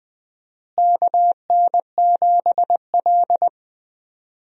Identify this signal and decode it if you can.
Morse code, heard as a keyed tone, about 20 wpm, 700 Hz. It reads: KN7L
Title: USA Call Signs - 4 Characters - Repeated 3x - 20wpm